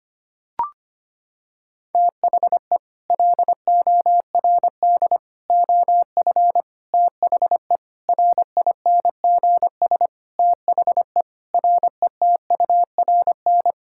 Morse code, heard as a keyed tone, about 25 wpm, 700 Hz. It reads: THE LORD OF THE RINGS THE RETURN
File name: Sets of 9 Words - 25wpm